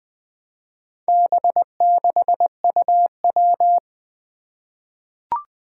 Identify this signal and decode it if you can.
Morse code, heard as a keyed tone, about 20 wpm, 700 Hz. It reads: B6UW
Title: Four Challenging Letter-Number Combinations 20wpm